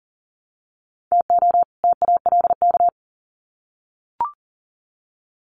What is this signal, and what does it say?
Morse code, heard as a keyed tone, about 40 wpm, 700 Hz. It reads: TO TALK
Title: Sets of 2 Words 40wpm